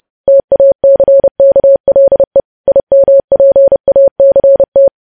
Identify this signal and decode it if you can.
Morse code, heard as a keyed tone, about 30 wpm, 550 Hz. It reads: TACKLE IMPACT
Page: news headlines in morse code at 30 wpm